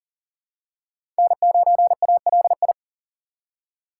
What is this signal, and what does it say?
Morse code, heard as a keyed tone, about 40 wpm, 700 Hz. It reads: N9ALI
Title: USA Call Signs - 5 Characters - 4x Character Spacing - 40wpm